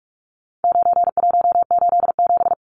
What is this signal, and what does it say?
Morse code, heard as a keyed tone, about 44 wpm, 700 Hz. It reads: 9187
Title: Numbers - 4-Digit 45wpm